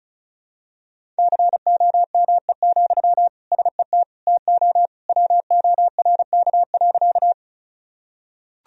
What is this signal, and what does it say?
Morse code, heard as a keyed tone, about 34 wpm, 700 Hz. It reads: COME, SET TO WORK.
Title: Sentences from Top 1000 Words - Review Part 1 - 35wpm